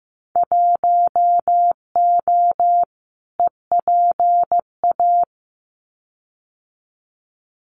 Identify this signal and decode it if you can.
Morse code, heard as a keyed tone, about 15 wpm, 700 Hz. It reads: 1O EPA
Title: Winter Field Day Exchanges 15wpm